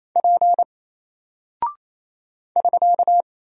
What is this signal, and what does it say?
Morse code, heard as a keyed tone, about 27 wpm, 700 Hz. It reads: P <SK>
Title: Single Character (Letter:Number:Punctuation:Prosign) Alt 28wpm